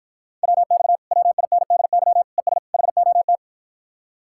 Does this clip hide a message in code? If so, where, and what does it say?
Yes: Morse code from 0.4 to 3.4 s, heard as a keyed tone, about 52 wpm, 700 Hz. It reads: WX WINDY ES HOT